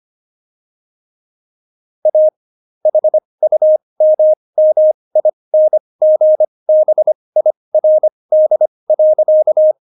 Morse code, heard as a keyed tone, about 25 wpm, spaced as for 20 wpm, 620 Hz. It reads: A HUMMINGBIRD.
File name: Jokes-25-char-20-eff